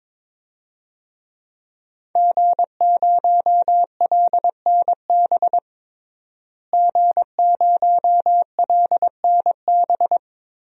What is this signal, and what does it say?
Morse code, heard as a keyed tone, about 21 wpm, 700 Hz. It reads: G0LNB G0LNB
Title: DXCC-100 Call Signs - 5 Characters - Repeated 3x - 22wpm